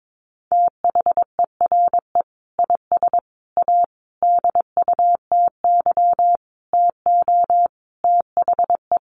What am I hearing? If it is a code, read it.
Morse code, heard as a keyed tone, about 22 wpm, 700 Hz. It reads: THERE IS A DUTY TO THE
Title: Sentences from Top 1500 Words 22wpm